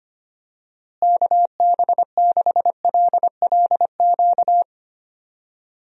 Morse code, heard as a keyed tone, about 25 wpm, 700 Hz. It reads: KB6LLQ
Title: USA Call Signs - 6 Characters - Repeated 3x - 25wpm